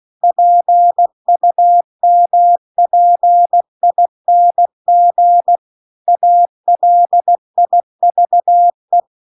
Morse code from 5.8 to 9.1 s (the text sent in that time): ALIVE